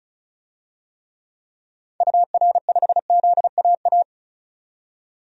Morse code, heard as a keyed tone, about 34 wpm, 700 Hz. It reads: UR5ZAA
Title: DXCC-100 Call Signs - 6 Characters - 35wpm